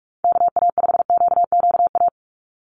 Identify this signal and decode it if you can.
Morse code, heard as a keyed tone, about 44 wpm, 700 Hz. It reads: KA5QQA
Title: NA SOTA Chaser Callsigns 45wpm